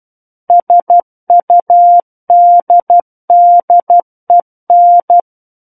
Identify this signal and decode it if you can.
Morse code, heard as a keyed tone, about 12 wpm, 700 Hz. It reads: SUDDEN
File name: Book One / 03-Chapter 3 - 12 WPM